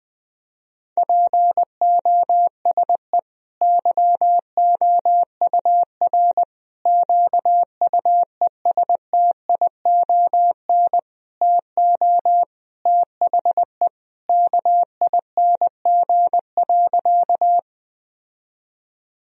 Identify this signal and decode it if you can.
Morse code, heard as a keyed tone, about 20 wpm, 700 Hz. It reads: POSE YOUR QUESTION TO THE KING.
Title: Sentences from Top 1000 Words - Review Part 2 - 20wpm